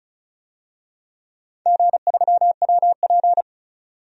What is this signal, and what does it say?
Morse code, heard as a keyed tone, about 34 wpm, 700 Hz. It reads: G3WP
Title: DXCC-100 Call Signs - 4 Characters - 35wpm